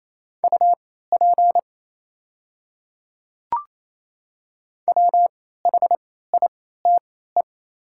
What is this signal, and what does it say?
Morse code, heard as a keyed tone, about 29 wpm, spaced as for 16 wpm, 700 Hz. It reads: UP WHITE
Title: 3x Character Spacing - Top 300 Words - 28wpm